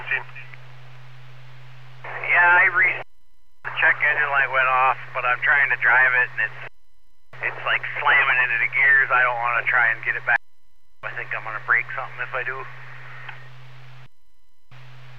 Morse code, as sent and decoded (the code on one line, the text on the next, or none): none